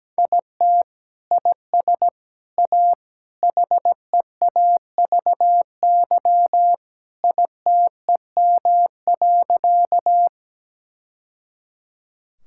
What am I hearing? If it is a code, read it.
Morse code, heard as a keyed tone, about 17 wpm, 700 Hz. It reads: IT IS A HEAVY ITEM.
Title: Sentences from Top 1000 Words - Review Part 6 - 17wpm